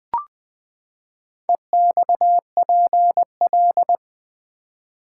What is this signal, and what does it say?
Morse code, heard as a keyed tone, about 20 wpm, 700 Hz. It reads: EXPL
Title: Build A Word - 4 Letters At A Time - 12-Letter Words - 20wpm